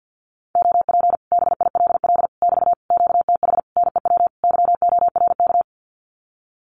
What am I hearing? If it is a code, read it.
Morse code, heard as a keyed tone, about 50 wpm, 700 Hz. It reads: OP BILL <BT> QTH NEW YORK
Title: Common QSO Phrases Encore 50wpm